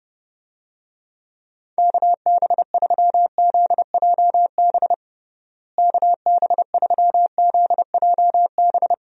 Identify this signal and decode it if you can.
Morse code, heard as a keyed tone, about 30 wpm, 700 Hz. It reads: KB3ZJB KB3ZJB
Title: USA Call Signs - 6 Characters - Repeated 3x - 30wpm